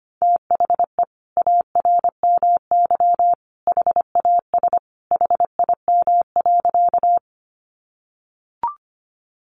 Morse code, heard as a keyed tone, about 25 wpm, 700 Hz. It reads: THE ARMY HAS HIM.
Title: Sentences from Top 800 Words 25wpm